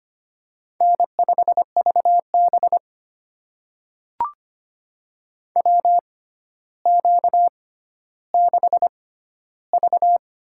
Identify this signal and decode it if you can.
Morse code, heard as a keyed tone, about 26 wpm, spaced as for 13 wpm, 700 Hz. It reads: N5VB WQ6V
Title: USA Call Signs - 4 Characters - 6x Character Spacing - 25wpm